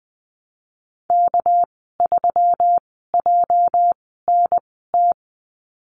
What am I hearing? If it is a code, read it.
Morse code, heard as a keyed tone, about 20 wpm, spaced as for 15 wpm, 700 Hz. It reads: K3JNT
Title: 207 Lesson 33g - Letter j - Call Signs 20wpm